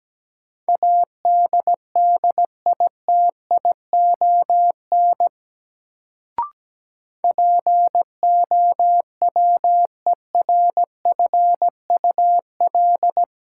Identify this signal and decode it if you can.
Morse code, heard as a keyed tone, about 17 wpm, 700 Hz. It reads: ADDITION POWERFUL
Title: Sentences from Top 1000 Words - Review Part 6 - New Word Drill - 17wpm